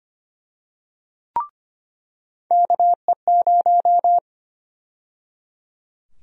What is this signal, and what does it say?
Morse code, heard as a keyed tone, about 25 wpm, 700 Hz. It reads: KE0
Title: USA Call Sign Prefixes - 25wpm